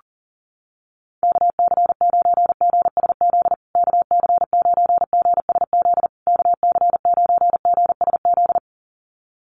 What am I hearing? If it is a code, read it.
Morse code, heard as a keyed tone, about 40 wpm, 700 Hz. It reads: KC9GSZ KC9GSZ KC9GSZ
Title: USA Call Signs Repeated 3x 40wpm